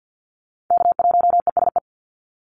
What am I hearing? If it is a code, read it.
Morse code, heard as a keyed tone, about 50 wpm, 700 Hz. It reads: K1ESE
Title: Call Signs - 50wpm